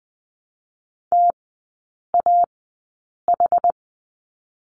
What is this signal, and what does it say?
Morse code, heard as a keyed tone, about 20 wpm, spaced as for 8 wpm, 700 Hz. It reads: TAH
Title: 060 Lesson 11e - Letter h - Three Characters 20wpm